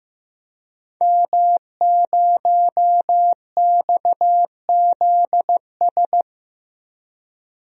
Morse code, heard as a keyed tone, about 15 wpm, 700 Hz. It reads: M0XZS
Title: DXCC-100 Call Signs - 5 Characters - Repeated 2x - 15wpm